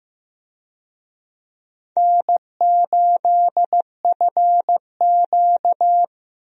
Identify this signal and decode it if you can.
Morse code, heard as a keyed tone, about 15 wpm, 700 Hz. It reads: N8FQ
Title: USA Call Signs - 4 Characters - Repeated 3x - 15wpm